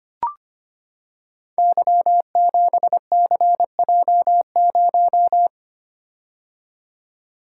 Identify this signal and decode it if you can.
Morse code, heard as a keyed tone, about 25 wpm, 700 Hz. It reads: Y7CJ0
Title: Code Groups - 5 Letter:Numbers - 25wpm